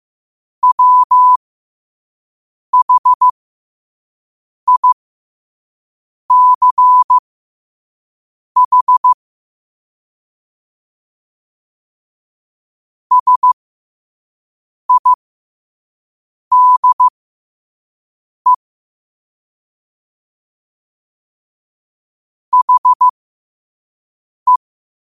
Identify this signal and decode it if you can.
Morse code, heard as a keyed tone, about 15 wpm, spaced as for 5 wpm, 1000 Hz. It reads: WHICH SIDE HE
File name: qotdfiles / 5wpm